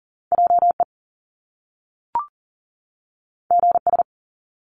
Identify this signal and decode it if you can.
Morse code, heard as a keyed tone, about 38 wpm, 700 Hz. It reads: JE GS
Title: Top Two Letter Combinations in Words 40wpm